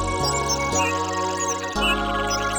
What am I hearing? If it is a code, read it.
There is no code.